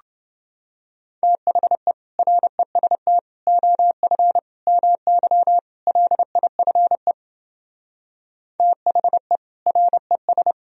Morse code, heard as a keyed tone, about 32 wpm, 700 Hz. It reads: THE REST OF MY LIFE THE RES